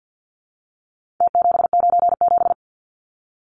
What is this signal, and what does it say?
Morse code, heard as a keyed tone, about 50 wpm, 700 Hz. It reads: T797
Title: Numbers - Contest Style - 4-Digit 50wpm